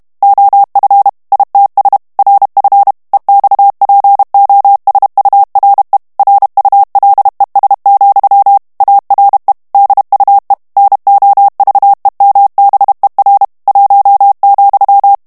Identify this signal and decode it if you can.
Morse code, heard as a keyed tone, about 32 wpm, 800 Hz. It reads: OF ITS RF EXPOSURE RULES, ARE DUE NOVEMBER 1,
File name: morse mp3 / ANGLAIS texte clair 3 vitesse 30 mots minute